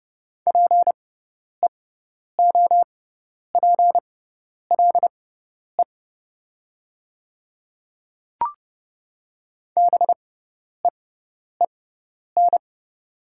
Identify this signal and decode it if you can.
Morse code, heard as a keyed tone, about 30 wpm, spaced as for 10 wpm, 700 Hz. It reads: PEOPLE BEEN
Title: 6x Character Spacing - Top 100 Words - 30wpm